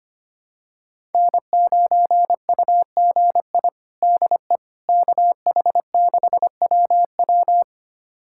Morse code, heard as a keyed tone, about 25 wpm, 700 Hz. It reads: N9UGI DE KH6WW
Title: USA Callsign DE Callsign - 5 Characters - 25wpm